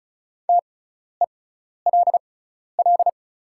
Morse code, heard as a keyed tone, about 36 wpm, spaced as for 12 wpm, 700 Hz. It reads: TELL
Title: 6x Character Spacing - Top 300 Words - 35wpm